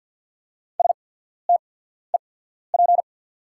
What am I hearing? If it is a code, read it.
Morse code, heard as a keyed tone, about 48 wpm, spaced as for 14 wpm, 690 Hz. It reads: STEP
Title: 8x Character Spacing - Top 500 Words - 50wpm